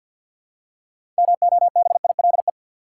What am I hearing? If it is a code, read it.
Morse code, heard as a keyed tone, about 52 wpm, 700 Hz. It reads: MOBILE
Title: Six Letter Words - 50wpm